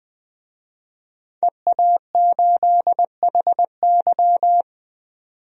Morse code, heard as a keyed tone, about 20 wpm, 700 Hz. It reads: EA8HY
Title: DXCC-100 Call Signs - 5 Characters - 20wpm